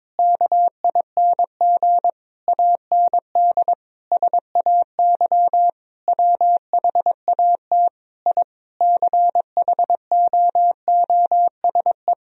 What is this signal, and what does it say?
Morse code, heard as a keyed tone, about 22 wpm, 700 Hz. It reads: KING AND SAY WHAT I CHOOSE
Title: Sentences from Top 1000 Words - Review Part 4 - 22wpm